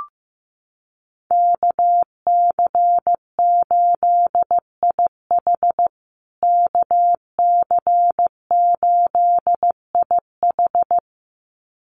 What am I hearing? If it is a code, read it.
Morse code, heard as a keyed tone, about 15 wpm, 700 Hz. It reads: KC8IH KC8IH
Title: USA Call Signs Repeated 2x 15wpm